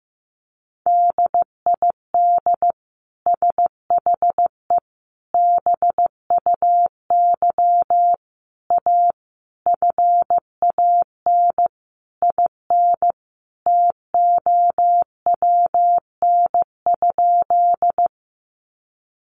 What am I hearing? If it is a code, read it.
Morse code, heard as a keyed tone, about 15 wpm, 700 Hz. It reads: DID SHE BUY A FAN IN TOWN?
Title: Sentences from Top 1300 Words 15wpm